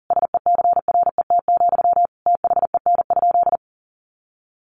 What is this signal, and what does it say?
Morse code, heard as a keyed tone, about 40 wpm, 700 Hz. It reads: SECRET, THEN?
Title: Sentences from Top 1200 Words 40wpm